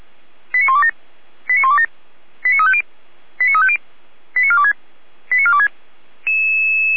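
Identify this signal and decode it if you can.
Five-tone selective call (ZVEI), heard as quick runs of five tones, tones of about 70 ms. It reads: 89127 89127 89350 89350 89426 89426
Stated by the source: ZVEI five-tone selective call received on an SDR, from feuerwehr-orlamuende.org (fire brigade Orlamünde).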